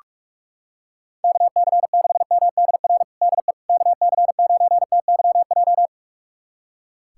Morse code, heard as a keyed tone, about 42 wpm, 700 Hz. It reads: KC6MDR DE KC9TYJ
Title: USA Callsign DE Callsign - 6 Characters - 45wpm